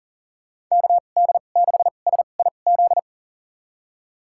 Morse code, heard as a keyed tone, about 37 wpm, spaced as for 28 wpm, 700 Hz. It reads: KD6SIZ
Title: USA Call Signs - 6 Characters - 2x Character Spacing - 40wpm